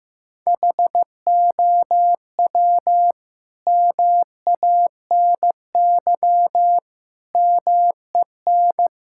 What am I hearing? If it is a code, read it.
Morse code, heard as a keyed tone, about 15 wpm, 700 Hz. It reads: HOW MANY MEN
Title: Sentences from Top 1600 Words 15wpm